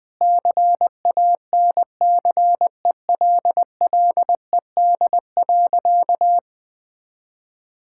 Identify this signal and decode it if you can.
Morse code, heard as a keyed tone, about 20 wpm, 700 Hz. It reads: CANCELLED.
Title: Sentences with 8-Word Phrases - Spoken Beginning - 20wpm